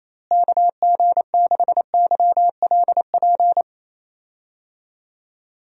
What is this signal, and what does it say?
Morse code, heard as a keyed tone, about 28 wpm, 700 Hz. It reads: KG6YLP
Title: USA Call Signs - 6 Characters - Repeated 3x - 28wpm